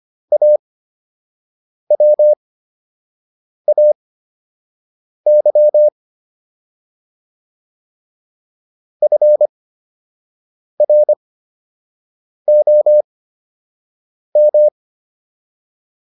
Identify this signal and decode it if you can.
Morse code, heard as a keyed tone, about 25 wpm, spaced as for 6 wpm, 600 Hz. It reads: AWAY FROM